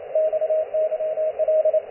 Morse code, heard as a keyed tone, about 28 wpm, 630 Hz, with radio noise. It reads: XYL